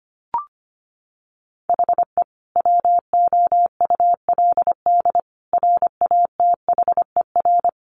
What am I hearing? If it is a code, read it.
Morse code, heard as a keyed tone, about 25 wpm, 700 Hz. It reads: HE WOULD RATHER
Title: Sentences from Top 1500 Words 25wpm